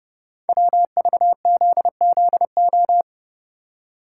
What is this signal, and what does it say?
Morse code, heard as a keyed tone, about 30 wpm, 700 Hz. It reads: WVZZO
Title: Code Groups - 5 Letters - 30wpm